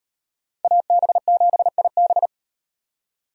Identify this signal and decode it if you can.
Morse code, heard as a keyed tone, about 38 wpm, 700 Hz. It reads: AB7IB